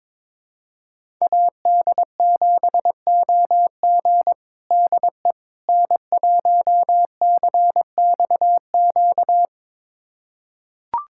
Morse code, heard as a keyed tone, about 22 wpm, 700 Hz. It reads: AD7OG DE N1CXQ
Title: USA Callsign DE Callsign - 5 Characters - 22wpm